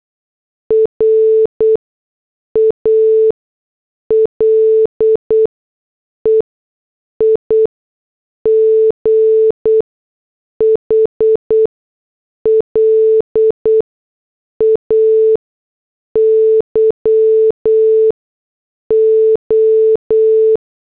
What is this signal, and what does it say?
Morse code, heard as a keyed tone, about 8 wpm, spaced as for 6 wpm, 440 Hz. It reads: RALEIGH LAYO
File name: morse / headlines